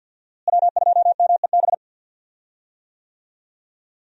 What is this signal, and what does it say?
Morse code, heard as a keyed tone, about 53 wpm, 700 Hz. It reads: W2MEB
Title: USA Call Signs - 5 Characters - Repeated 2x - 50wpm